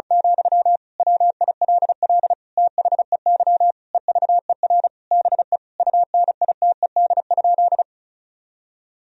Morse code, heard as a keyed tone, about 34 wpm, 700 Hz. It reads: , WILL THEY EVER BE UNITED?
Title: Sentences from Top 1000 Words - Review Part 4 - 35wpm